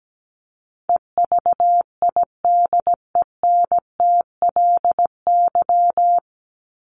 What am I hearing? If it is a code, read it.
Morse code, heard as a keyed tone, about 17 wpm, 700 Hz. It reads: EVIDENTLY
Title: LY Suffix - Top 100 Words with Base Words 17wpm